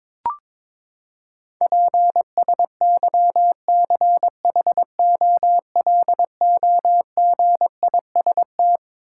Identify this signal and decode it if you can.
Morse code, heard as a keyed tone, about 22 wpm, 700 Hz. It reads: PSYCHOLOGIST
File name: Twelve Letter Words - 22wpm